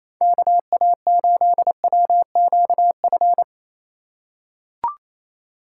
Morse code, heard as a keyed tone, about 28 wpm, 700 Hz. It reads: KA8WQF
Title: USA Call Signs - 6 Characters - 4x Character Spacing - 28wpm